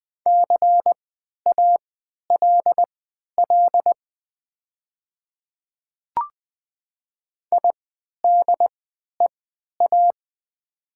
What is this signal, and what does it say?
Morse code, heard as a keyed tone, about 20 wpm, spaced as for 11 wpm, 700 Hz. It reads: CALL IDEA
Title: 3x Character Spacing - Top 300 Words - 20wpm